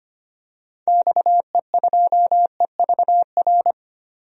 Morse code, heard as a keyed tone, about 25 wpm, 690 Hz. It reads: XE2EVR
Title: DXCC-100 Call Signs - 6 Characters - 25wpm